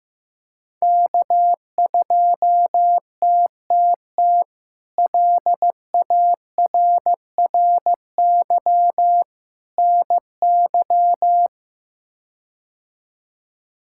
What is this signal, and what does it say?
Morse code, heard as a keyed tone, about 15 wpm, 700 Hz. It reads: K2TTT LARRY NY
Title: K1USN SST QSO Practice - 15wpm